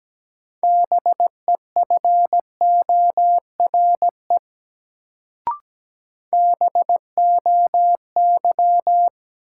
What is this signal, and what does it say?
Morse code, heard as a keyed tone, about 17 wpm, 700 Hz. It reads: BEFORE BOY
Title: Sentences from Top 200 Words - New Word Drill - 17wpm